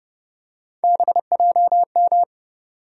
Morse code, heard as a keyed tone, about 30 wpm, 700 Hz. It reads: BJM
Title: USA Call Sign Postfixes - 3 Letters 30wpm